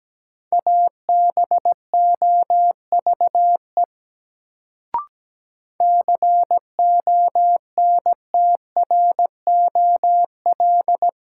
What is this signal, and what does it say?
Morse code, heard as a keyed tone, about 17 wpm, 700 Hz. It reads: ABOVE CONTROL
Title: Sentences from Top 1000 Words - Review Part 2 - New Word Drill - 17wpm